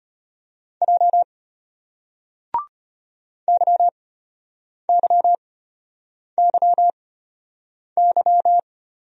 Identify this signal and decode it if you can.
Morse code, heard as a keyed tone, about 34 wpm, spaced as for 8 wpm, 700 Hz. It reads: J YYYY